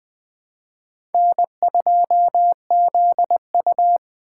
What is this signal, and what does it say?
Morse code, heard as a keyed tone, about 20 wpm, 700 Hz. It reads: N2ZU
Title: Speed Racing - USA 1x2 Call Signs 17wpm